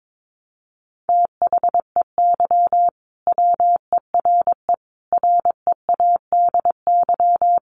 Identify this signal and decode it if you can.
Morse code, heard as a keyed tone, about 22 wpm, 700 Hz. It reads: THEY WERE READY
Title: A Sound Of Thunder 22wpm